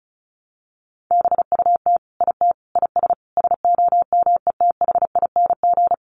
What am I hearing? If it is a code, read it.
Morse code, heard as a keyed tone, about 34 wpm, 700 Hz. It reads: BUT IT IS SOMETHING
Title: Sentences from Top 1000 Words 35wpm